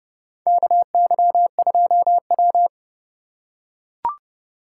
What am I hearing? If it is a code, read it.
Morse code, heard as a keyed tone, about 30 wpm, 700 Hz. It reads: KY2W
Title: USA Call Signs - 4x Character Spacing - 30wpm